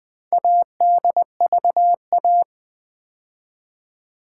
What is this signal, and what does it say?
Morse code, heard as a keyed tone, about 20 wpm, 700 Hz. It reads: ADVA
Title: Build A Word - 2 Letters At A Time - 8-Letter Words - 20wpm